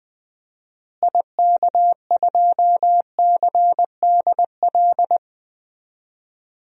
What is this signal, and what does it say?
Morse code, heard as a keyed tone, about 20 wpm, 700 Hz. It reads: IK2CDL
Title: DXCC-100 Call Signs - 6 Characters - Repeated 3x - 20wpm